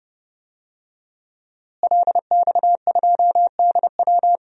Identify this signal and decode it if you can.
Morse code, heard as a keyed tone, about 30 wpm, 700 Hz. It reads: LX2DW
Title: DXCC-100 Call Signs - 5 Characters - Repeated 2x - 30wpm